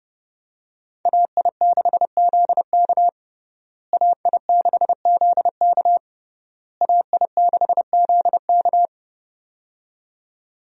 Morse code, heard as a keyed tone, about 30 wpm, 700 Hz. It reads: AI6ZK AI6ZK AI6ZK
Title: USA Call Signs - 5 Characters - Repeated 3x - 30wpm